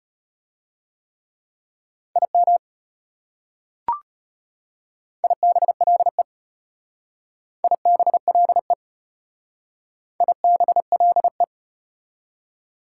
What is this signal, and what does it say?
Morse code, heard as a keyed tone, about 33 wpm, 700 Hz. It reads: IM IBLE IBLE IBLE